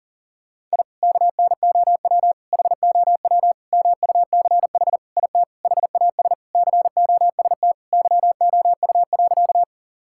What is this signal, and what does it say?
Morse code, heard as a keyed tone, about 37 wpm, 700 Hz. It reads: I KNOW HOW MUCH IT HAS COST YOU.
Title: Sentences from Top 1000 Words - Review Part 5 - 40wpm